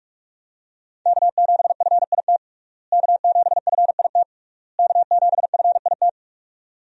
Morse code, heard as a keyed tone, about 46 wpm, 700 Hz. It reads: K7FIT K7FIT K7FIT
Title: USA Call Signs - 5 Characters - Repeated 3x - 45wpm